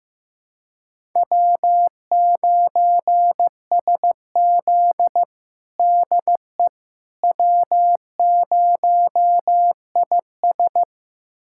Morse code, heard as a keyed tone, about 16 wpm, 700 Hz. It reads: W9SZ DE W0IS
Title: USA Callsign DE Callsign - 4 Characters - 15wpm